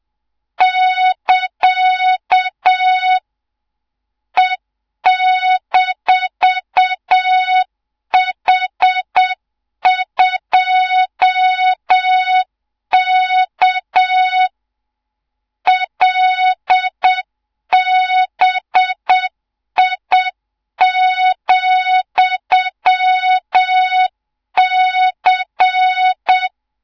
Morse code, heard as a keyed tone, about 7 wpm, 3700 Hz, with noise. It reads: <KA> E-H2K LBI,C
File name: Morze-HA2MN / hang / 24 GYAKORLAT